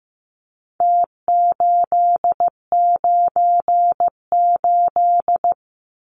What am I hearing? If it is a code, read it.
Morse code, heard as a keyed tone, about 15 wpm, 700 Hz. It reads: T898
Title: Numbers - Contest Style - 4-Digit 15wpm